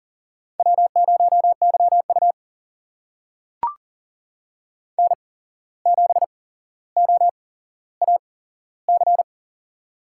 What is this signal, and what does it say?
Morse code, heard as a keyed tone, about 40 wpm, spaced as for 16 wpm, 700 Hz. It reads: W0YU N7OAC
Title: USA Call Signs - 8x Character Spacing - 40wpm